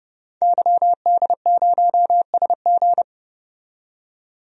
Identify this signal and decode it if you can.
Morse code, heard as a keyed tone, about 32 wpm, 700 Hz. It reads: YD0SG
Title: DXCC-100 Call Signs - 5 Characters - 30wpm